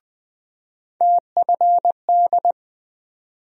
Morse code, heard as a keyed tone, about 20 wpm, 700 Hz. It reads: TFD
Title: USA Call Sign Postfixes - 3 Letters 20wpm